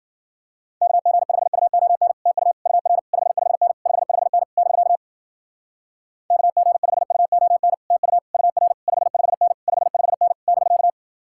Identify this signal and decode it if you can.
Morse code, heard as a keyed tone, about 52 wpm, 700 Hz. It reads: KG5UON TU UR 55N 55N <BK> KG5UON TU UR 55N 55N <BK>